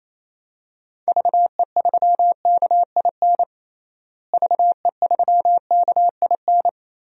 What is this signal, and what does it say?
Morse code, heard as a keyed tone, about 28 wpm, 700 Hz. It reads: VE3KIN VE3KIN
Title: DXCC-100 Call Signs - 6 Characters - Repeated 3x - 28wpm